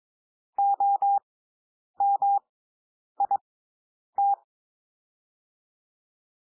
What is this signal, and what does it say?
Morse code, heard as a keyed tone, about 22 wpm, spaced as for 9 wpm, 800 Hz. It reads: OMIT
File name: SOUND WORDS 22 wpm